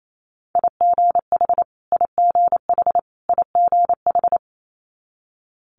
Morse code, heard as a keyed tone, about 28 wpm, 700 Hz. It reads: IGH IGH IGH
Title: Multi-letter Phonemes Repeated 3x 28wpm